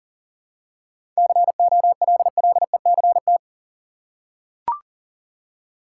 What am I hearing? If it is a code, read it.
Morse code, heard as a keyed tone, about 40 wpm, 700 Hz. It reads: COLLECT
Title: Seven Letter Words - 40wpm